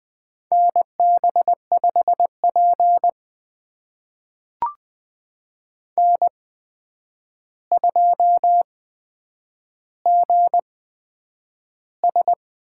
Morse code, heard as a keyed tone, about 20 wpm, spaced as for 9 wpm, 700 Hz. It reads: NB5P N2GS
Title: USA Call Signs - 4 Characters - 8x Character Spacing - 20wpm